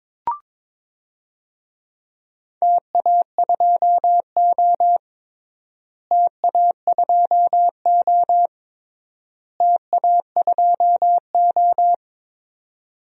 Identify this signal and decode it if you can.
Morse code, heard as a keyed tone, about 22 wpm, 700 Hz. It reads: TA2O TA2O TA2O
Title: DXCC-100 Call Signs - 4 Characters - Repeated 3x - 22wpm